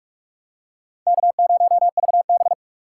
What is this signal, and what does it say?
Morse code, heard as a keyed tone, about 44 wpm, 700 Hz. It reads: K0VB
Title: USA Call Signs - 2x Character Spacing - 45wpm